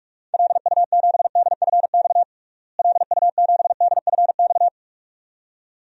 Morse code, heard as a keyed tone, about 44 wpm, 700 Hz. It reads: LU7DFX LU7DFX
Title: DXCC-100 Call Signs - 6 Characters - Repeated 2x - 45wpm